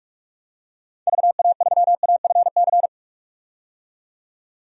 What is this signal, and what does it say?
Morse code, heard as a keyed tone, about 44 wpm, 700 Hz. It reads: VA3AFC
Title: DXCC-100 Call Signs - 6 Characters - 45wpm